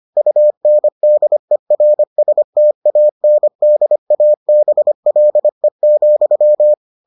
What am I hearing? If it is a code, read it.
Morse code, heard as a keyed tone, about 25 wpm, 600 Hz. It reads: UNDERSTANDABLE,